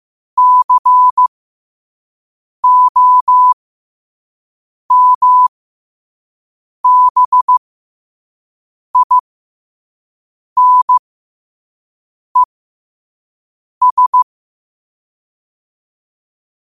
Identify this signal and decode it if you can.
Morse code, heard as a keyed tone, about 15 wpm, spaced as for 5 wpm, 1000 Hz. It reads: COMBINES
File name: qotdfiles / 5wpm